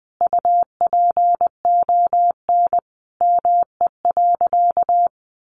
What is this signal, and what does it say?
Morse code, heard as a keyed tone, about 20 wpm, 700 Hz. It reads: UPON ME.
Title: Sentences from Top 1300 Words 20wpm